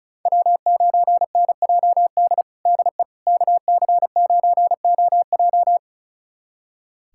Morse code, heard as a keyed tone, about 34 wpm, 700 Hz. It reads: W9NJD DE KC9OJ
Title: USA Callsign DE Callsign - 5 Characters - 35wpm